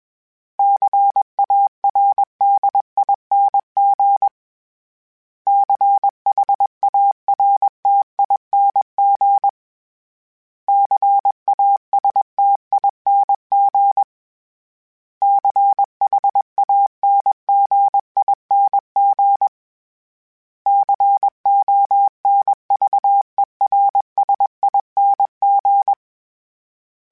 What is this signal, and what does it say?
Morse code, heard as a keyed tone, about 21 wpm, 800 Hz. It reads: CARDING CHARTING CASTING CHANGING CONVERSING
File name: Special INGs / Special INGs-mp3s / Sngl Ing @20